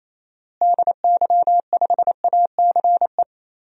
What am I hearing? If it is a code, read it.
Morse code, heard as a keyed tone, about 28 wpm, 700 Hz. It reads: DY5ACE
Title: DXCC-100 Call Signs - 6 Characters - Repeated 2x - 28wpm